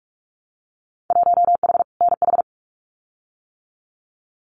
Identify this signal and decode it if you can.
Morse code, heard as a keyed tone, about 44 wpm, 700 Hz. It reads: 1H NH
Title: Winter Field Day Exchanges 45wpm